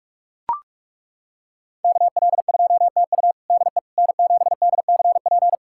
Morse code, heard as a keyed tone, about 41 wpm, 700 Hz. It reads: KL2TU DE N7DCP
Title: USA Callsign DE Callsign - 5 Characters - 45wpm